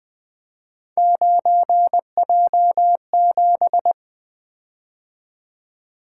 Morse code, heard as a keyed tone, about 20 wpm, 700 Hz. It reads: 9J7
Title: Three Letter-Number Combinations - 20wpm